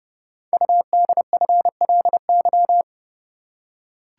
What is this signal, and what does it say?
Morse code, heard as a keyed tone, about 29 wpm, 690 Hz. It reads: UDFLY
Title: Code Groups - 5 Letters - 30wpm